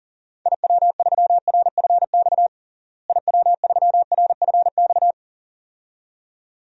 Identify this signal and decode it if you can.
Morse code, heard as a keyed tone, about 37 wpm, 700 Hz. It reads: IW3RFX IW3RFX
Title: DXCC-100 Call Signs - 6 Characters - Repeated 2x - 40wpm